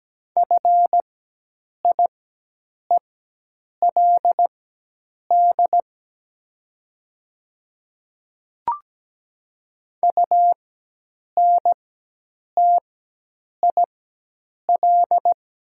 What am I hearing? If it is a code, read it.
Morse code, heard as a keyed tone, about 17 wpm, spaced as for 8 wpm, 700 Hz. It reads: FIELD UNTIL
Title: 4x Character Spacing - Top 500 Words - 17wpm